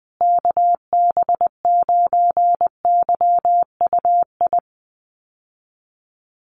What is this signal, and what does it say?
Morse code, heard as a keyed tone, about 20 wpm, 700 Hz. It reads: KB9YUI
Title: USA Call Signs Repeated 3x 20wpm